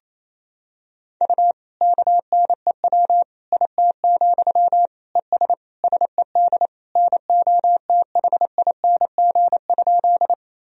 Morse code, heard as a keyed tone, about 28 wpm, 700 Hz. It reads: U KNEW IT, ES SED NOTHING?
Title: Sentences from Top 500 Words and Abbreviations 28wpm